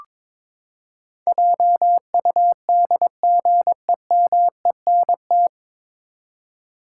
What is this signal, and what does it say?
Morse code, heard as a keyed tone, about 22 wpm, 700 Hz. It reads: JUDGEMENT
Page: Morse Code Ninja practice 22wpm